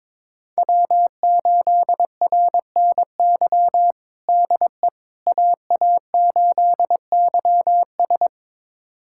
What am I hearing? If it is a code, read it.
Morse code, heard as a keyed tone, about 21 wpm, 700 Hz. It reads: W8RNY DE AA8YS
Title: USA Callsign DE Callsign 22wpm